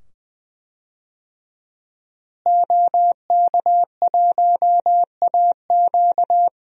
Morse code, heard as a keyed tone, about 20 wpm, 700 Hz. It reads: OK1AQ
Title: DXCC-100 Call Signs - 5 Characters - Repeated 2x - 20wpm